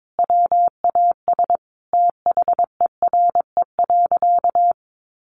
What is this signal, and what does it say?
Morse code, heard as a keyed tone, about 22 wpm, 700 Hz. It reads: WAS THERE.
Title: Sentences from Top 1400 Words 22wpm